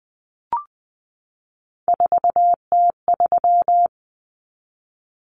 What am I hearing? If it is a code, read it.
Morse code, heard as a keyed tone, about 20 wpm, 700 Hz. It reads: 4T3
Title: Common Cut Numbers - 3-Digit 20wpm